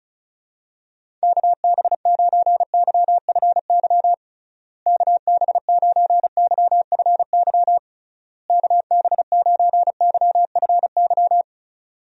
Morse code, heard as a keyed tone, about 34 wpm, 700 Hz. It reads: KB9YFY KB9YFY KB9YFY
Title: USA Call Signs - 6 Characters - Repeated 3x - 35wpm